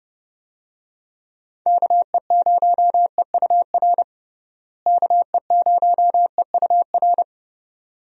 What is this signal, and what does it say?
Morse code, heard as a keyed tone, about 30 wpm, 700 Hz. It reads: KE0EUR KE0EUR
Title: USA Call Signs - 6 Characters - Repeated 2x - 30wpm